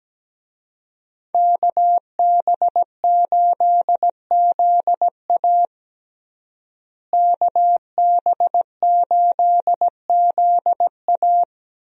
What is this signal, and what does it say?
Morse code, heard as a keyed tone, about 17 wpm, 700 Hz. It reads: KB8ZA KB8ZA
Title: USA Call Signs - 5 Characters - Repeated 3x - 17wpm